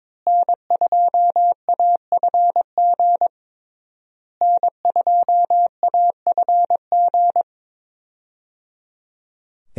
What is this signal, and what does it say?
Morse code, heard as a keyed tone, about 21 wpm, 700 Hz. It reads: N2AFG N2AFG
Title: USA Call Signs - 5 Characters - Repeated 3x - 22wpm